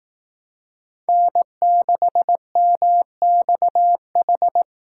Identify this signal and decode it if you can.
Morse code, heard as a keyed tone, about 18 wpm, 700 Hz. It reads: N6MXH
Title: Speed Racing - USA 1x3 Call Signs 15wpm